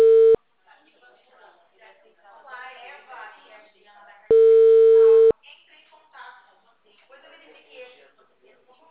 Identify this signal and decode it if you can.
Ringback tone (China, 450 Hz), with speech.
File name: out- -2009- - - .91